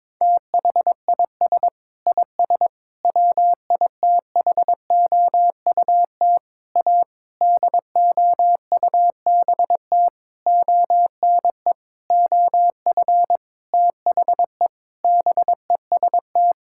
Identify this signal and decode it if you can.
Morse code, heard as a keyed tone, about 22 wpm, 700 Hz. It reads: THIS IS WITHOUT A DOUBT ONE OF THE BEST
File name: Sentences with 7-Word Phrases - 22wpm